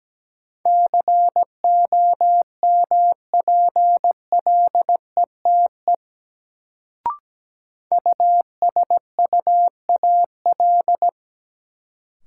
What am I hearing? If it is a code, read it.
Morse code, heard as a keyed tone, about 17 wpm, 700 Hz. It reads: COMPLETE USUAL
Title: Sentences from Top 400 Words - New Word Drill - 17wpm